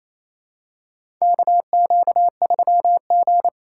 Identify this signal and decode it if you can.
Morse code, heard as a keyed tone, about 28 wpm, 700 Hz. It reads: KQ3G
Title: USA Call Signs - 2x Character Spacing - 28wpm